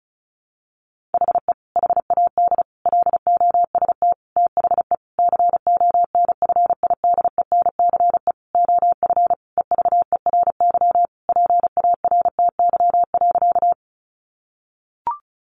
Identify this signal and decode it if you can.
Morse code, heard as a keyed tone, about 34 wpm, 700 Hz. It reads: HE HAD LOST THE CONFIDENCE OF EVERY PARTY.
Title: Sentences from Top 1500 Words 35wpm